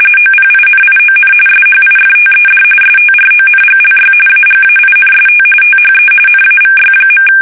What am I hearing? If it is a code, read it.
RTTY, about 45.45 baud. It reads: OPINION SHIFTED STRONGLY AGAINST THE NAVY.
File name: fsk850 / usnavy5